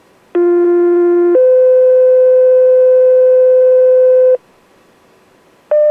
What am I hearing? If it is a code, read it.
Two-tone page, heard as a one-second tone, then a three-second tone, about 350 then 510 Hz. It is heard from 0.4 to 4.4 s.